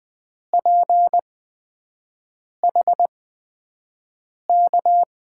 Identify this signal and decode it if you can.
Morse code, heard as a keyed tone, about 21 wpm, spaced as for 6 wpm, 700 Hz. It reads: PH K
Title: USA Call Signs - 8x Character Spacing - 20wpm